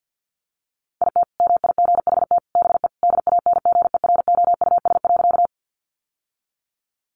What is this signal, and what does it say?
Morse code, heard as a keyed tone, about 50 wpm, 700 Hz. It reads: IT MIGHT BE DANGEROUS.
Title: Sentences from Top 1200 Words 50wpm